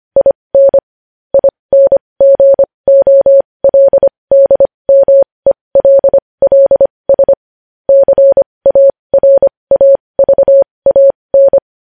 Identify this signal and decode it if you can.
Morse code, heard as a keyed tone, about 25 wpm, spaced as for 20 wpm, 550 Hz. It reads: IN INGOLDMELLS CARAVAN
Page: News Headlines in Morse Code at 20 WPM